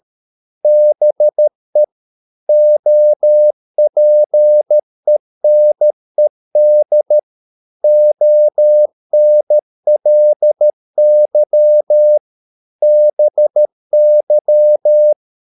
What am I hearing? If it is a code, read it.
Morse code, heard as a keyed tone, about 13 wpm, 600 Hz. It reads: BE OPENED ONLY BY